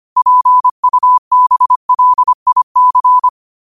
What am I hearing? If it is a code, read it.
Morse code, heard as a keyed tone, about 25 wpm, 1000 Hz. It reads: PUBLIC
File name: qotdfiles / 25wpm